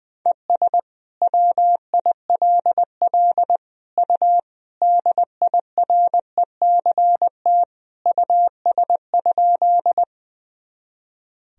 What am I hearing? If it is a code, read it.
Morse code, heard as a keyed tone, about 20 wpm, 700 Hz. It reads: ES WILL U DIRECT US?
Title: Sentences from Top 400 Words and Abbreviations 20wpm